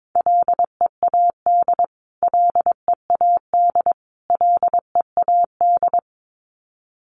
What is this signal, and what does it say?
Morse code, heard as a keyed tone, about 22 wpm, 700 Hz. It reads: LEAD LEAD LEAD
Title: Top 500 Words Repeated 22wpm